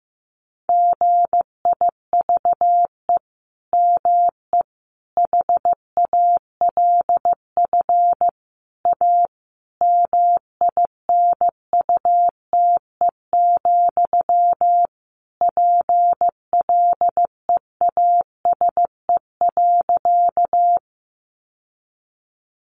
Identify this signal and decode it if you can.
Morse code, heard as a keyed tone, about 15 wpm, 700 Hz. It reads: GIVE ME HALF A MINUTE, PLEASE.
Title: Sentences from Top 1500 Words 15wpm